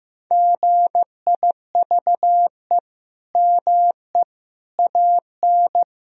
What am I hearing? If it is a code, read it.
Morse code, heard as a keyed tone, about 15 wpm, 700 Hz. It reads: GIVE ME AN
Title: Sentences from Top 1000 Words - Review Part 1 - 15wpm